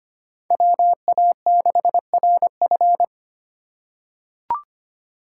Morse code, heard as a keyed tone, about 25 wpm, 700 Hz. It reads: WA6RF
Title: USA Call Signs - 2x Character Spacing - 25wpm